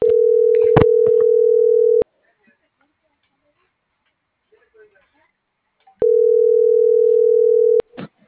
Ringback tone (North America), 440 + 480 Hz.
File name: CENTRAL TERRESTRE / out lucero.sky